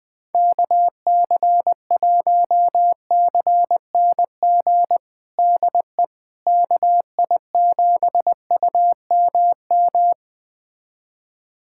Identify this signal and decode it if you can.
Morse code, heard as a keyed tone, about 20 wpm, 700 Hz. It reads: KC1CNG DE KI7UMM
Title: USA Callsign DE Callsign 20wpm